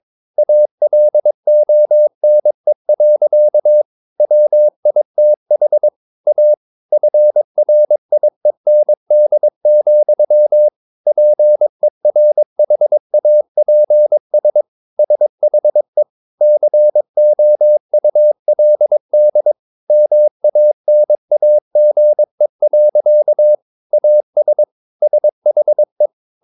Morse code, heard as a keyed tone, about 22 wpm, 600 Hz. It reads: ALONE. WITH A FRIEND, PERHAPS SHE COULD MANAGE. AS SHE
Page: Romanen Never af Ken Follet på engelsk.